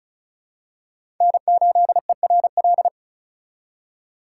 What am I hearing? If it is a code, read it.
Morse code, heard as a keyed tone, about 36 wpm, 700 Hz. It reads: N8ERL
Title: USA Call Signs - 5 Characters - 4x Character Spacing - 35wpm